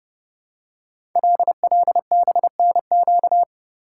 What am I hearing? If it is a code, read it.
Morse code, heard as a keyed tone, about 29 wpm, 700 Hz. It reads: LLBNQ
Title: Code Groups - 5 Letters - 30wpm